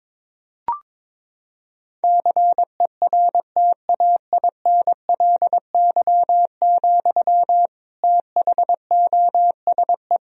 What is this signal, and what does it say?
Morse code, heard as a keyed tone, about 22 wpm, 700 Hz. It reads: CERTAINLY, THOSE
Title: Sentences from Top 1000 Words - Review Part 2 - 22wpm